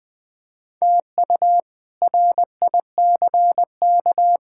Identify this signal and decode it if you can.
Morse code, heard as a keyed tone, about 20 wpm, 700 Hz. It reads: TU RICK
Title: K1USN SST QSO Practice - 20wpm